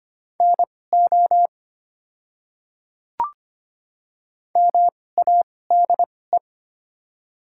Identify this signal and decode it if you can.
Morse code, heard as a keyed tone, about 25 wpm, spaced as for 18 wpm, 700 Hz. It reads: NO MADE
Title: 2x Character Spacing - Top 300 Words - 25wpm